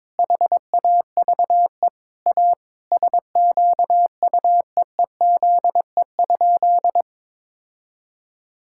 Morse code, heard as a keyed tone, about 22 wpm, 700 Hz. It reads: HAVE A SQUEEZE?
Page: morse-code-ninja-practice-22wpm